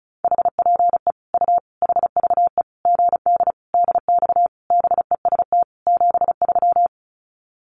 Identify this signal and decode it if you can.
Morse code, heard as a keyed tone, about 34 wpm, 700 Hz. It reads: HPE U HVE GD DX BEST 73
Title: Common QSO Phrases 35wpm